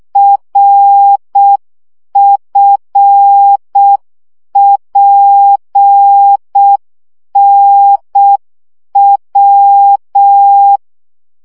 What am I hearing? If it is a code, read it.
Morse code, heard as a keyed tone, about 6 wpm, 800 Hz. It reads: RFPNW
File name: morze vizsga 2011-3 5